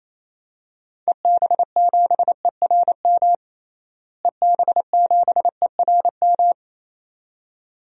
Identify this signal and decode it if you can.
Morse code, heard as a keyed tone, about 28 wpm, 700 Hz. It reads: EB7ERM EB7ERM
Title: DXCC-100 Call Signs - 6 Characters - Repeated 2x - 28wpm